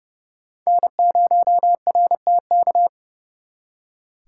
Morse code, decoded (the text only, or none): N0RTK